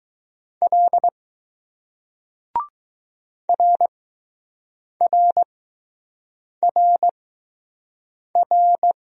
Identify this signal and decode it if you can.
Morse code, heard as a keyed tone, about 20 wpm, spaced as for 6 wpm, 700 Hz. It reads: L RRRR